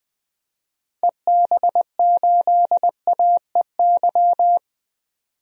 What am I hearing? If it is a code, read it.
Morse code, heard as a keyed tone, about 20 wpm, 700 Hz. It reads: EB8AEY
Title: DXCC-100 Call Signs - 6 Characters - 20wpm